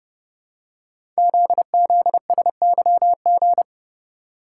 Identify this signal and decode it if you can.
Morse code, heard as a keyed tone, about 30 wpm, 700 Hz. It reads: ZZSYG
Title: Code Groups - 5 Letters - 30wpm